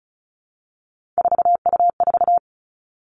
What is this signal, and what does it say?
Morse code, heard as a keyed tone, about 36 wpm, 700 Hz. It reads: 4U4
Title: Three Challenging Letter-Number Combinations 35wpm